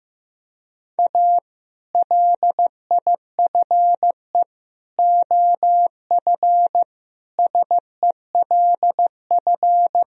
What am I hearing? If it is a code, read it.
Morse code, heard as a keyed tone, about 15 wpm, 700 Hz. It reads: A LIFE OF SELF
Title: Sentences from Top 1000 Words - Review Part 1 - 15wpm